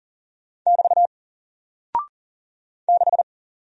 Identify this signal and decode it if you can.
Morse code, heard as a keyed tone, about 40 wpm, 700 Hz. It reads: <BT> 6
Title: Single Character (Letter:Number:Punctuation:Prosign) Alt 40wpm